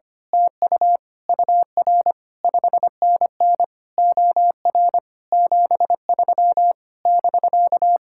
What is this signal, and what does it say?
Morse code, heard as a keyed tone, about 24 wpm, 700 Hz. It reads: TU UR 5NN OR 73 <BK>
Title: Speed-Racing Common QSO Phrases - 25wpm